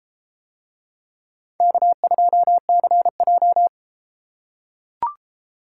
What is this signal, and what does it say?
Morse code, heard as a keyed tone, about 31 wpm, 700 Hz. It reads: K2CJ